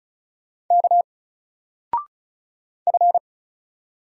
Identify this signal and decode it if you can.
Morse code, heard as a keyed tone, about 35 wpm, 700 Hz. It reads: K F
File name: Single-Letter-Number-35wpm